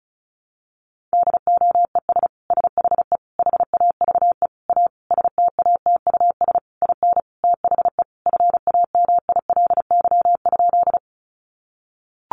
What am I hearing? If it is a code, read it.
Morse code, heard as a keyed tone, about 34 wpm, 700 Hz. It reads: DOES SHE HAVE A STATUS IN THE FAMILY?
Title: Sentences from Top 1400 Words 35wpm